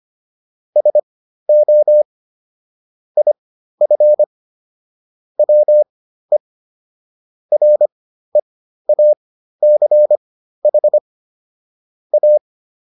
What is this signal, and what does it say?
Morse code, heard as a keyed tone, about 25 wpm, spaced as for 13 wpm, 600 Hz. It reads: SO IF WE REACH A